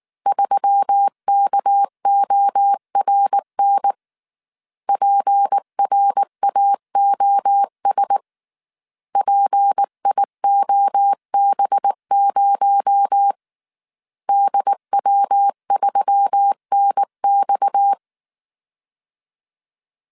Morse code, heard as a keyed tone, about 19 wpm, 800 Hz. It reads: 3KORN PRAOS PIOB0 DW3NX
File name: CW / DET / 1 6 1500C S A 18wpm 2255